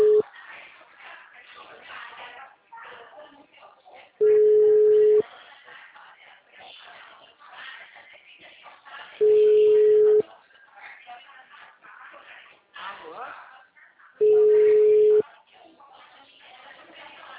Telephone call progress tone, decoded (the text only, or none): ringback tone